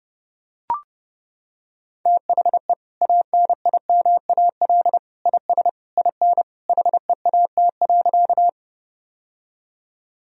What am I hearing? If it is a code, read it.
Morse code, heard as a keyed tone, about 30 wpm, 700 Hz. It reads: THE ANIMAL IS IN HEAT.
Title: Sentences from Top 1000 Words - Review Part 3 - 30wpm